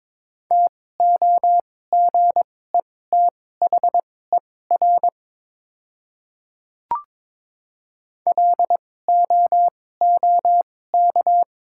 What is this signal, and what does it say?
Morse code, heard as a keyed tone, about 22 wpm, spaced as for 16 wpm, 700 Hz. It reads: TOGETHER LOOK
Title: 2x Character Spacing - Top 500 Words - 22wpm